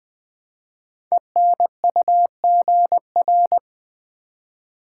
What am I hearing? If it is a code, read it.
Morse code, heard as a keyed tone, about 20 wpm, 700 Hz. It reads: ENUGR
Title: Code Groups - 5 Letters - 20wpm